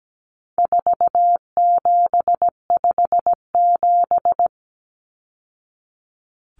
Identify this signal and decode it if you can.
Morse code, heard as a keyed tone, about 17 wpm, 700 Hz. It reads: 4757
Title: Numbers - 4-Digit 17wpm